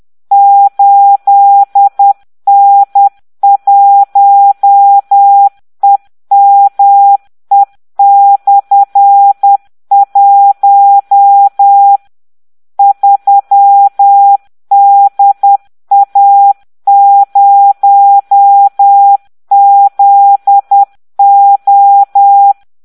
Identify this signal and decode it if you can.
Morse code, heard as a keyed tone, about 10 wpm, 800 Hz. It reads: 8N1EME/1 3DA0ZO